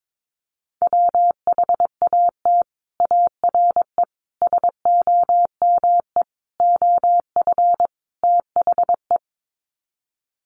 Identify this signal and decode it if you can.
Morse code, heard as a keyed tone, about 22 wpm, 700 Hz. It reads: WHAT ARE SOME OF THE
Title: Sets of 5 Words Encore 22wpm